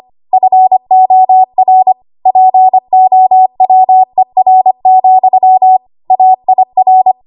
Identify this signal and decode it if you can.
Morse code, heard as a keyed tone, about 25 wpm, 750 Hz. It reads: FOR POWER, AIR
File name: Morse / 25WPM